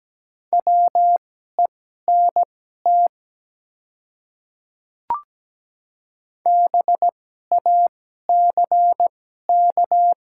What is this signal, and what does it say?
Morse code, heard as a keyed tone, about 17 wpm, spaced as for 12 wpm, 700 Hz. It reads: WENT BACK
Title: 2x Character Spacing - Top 200 Words - 17wpm